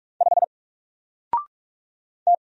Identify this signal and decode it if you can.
Morse code, heard as a keyed tone, about 44 wpm, 700 Hz. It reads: 5 T
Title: Single Character (Letter:Number:Punctuation:Prosign) 45wpm